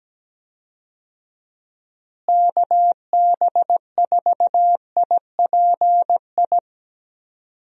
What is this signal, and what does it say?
Morse code, heard as a keyed tone, about 17 wpm, 700 Hz. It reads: KB4IPI
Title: USA Call Signs - 6 Characters - Repeated 2x - 17wpm